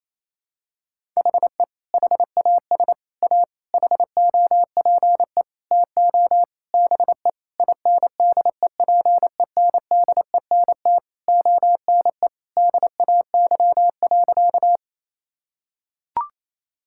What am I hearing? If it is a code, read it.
Morse code, heard as a keyed tone, about 28 wpm, 700 Hz. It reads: HE HAS A HOPE TO BE INDEPENDENT ONE DAY.
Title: Sentences from Top 1000 Words - Review Part 6 - 28wpm